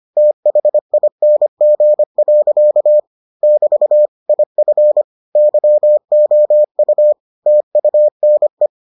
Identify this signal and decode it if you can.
Morse code, heard as a keyed tone, about 25 wpm, 600 Hz. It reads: THING. <BT> IF YOU TUNE